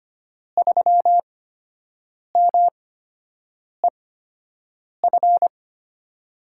Morse code, heard as a keyed tone, about 25 wpm, spaced as for 7 wpm, 700 Hz. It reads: 3MEF
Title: USA Call Signs - 5 Characters - 8x Character Spacing - 25wpm